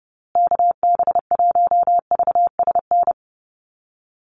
Morse code, heard as a keyed tone, about 29 wpm, 700 Hz. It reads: KB1VSN
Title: USA Call Signs Repeated 2x 30wpm